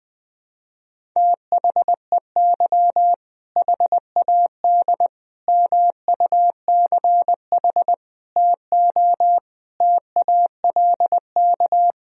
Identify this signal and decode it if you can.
Morse code, heard as a keyed tone, about 20 wpm, 700 Hz. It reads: THEY HAD MUCH TO TALK
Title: Sentences from Top 400 Words and Abbreviations 20wpm